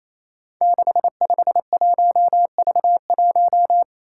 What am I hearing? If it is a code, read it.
Morse code, heard as a keyed tone, about 28 wpm, 700 Hz. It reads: 651V1
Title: Five Challenging Letter-Number Combinations 28wpm